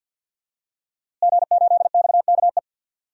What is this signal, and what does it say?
Morse code, heard as a keyed tone, about 50 wpm, 700 Hz. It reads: G8XKE
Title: DXCC-100 Call Signs - 5 Characters - Repeated 2x - 50wpm